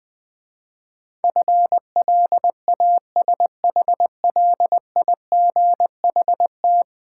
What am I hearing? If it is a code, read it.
Morse code, heard as a keyed tone, about 20 wpm, 700 Hz. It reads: FLASHLIGHT
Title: Easy Words - Speed-Racing with Hints - 20wpm